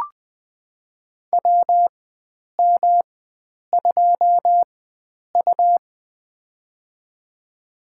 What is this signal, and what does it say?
Morse code, heard as a keyed tone, about 20 wpm, spaced as for 9 wpm, 700 Hz. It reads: WM2U
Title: USA Call Signs - 4 Characters - 4x Character Spacing - 20wpm